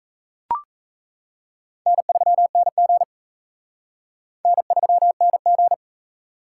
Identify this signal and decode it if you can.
Morse code, heard as a keyed tone, about 38 wpm, 700 Hz. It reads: N3NG N3NG